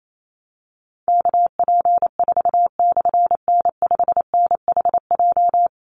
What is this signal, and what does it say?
Morse code, heard as a keyed tone, about 28 wpm, 700 Hz. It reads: KP4/N5NHJ
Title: Call Signs Repeated 3x - 28wpm